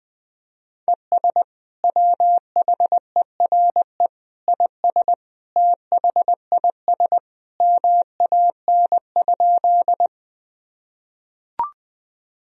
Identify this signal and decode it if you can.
Morse code, heard as a keyed tone, about 20 wpm, 700 Hz. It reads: ES WHERE IS THIS MAN?
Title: Sentences from Top 200 Words and Abbreviations 20wpm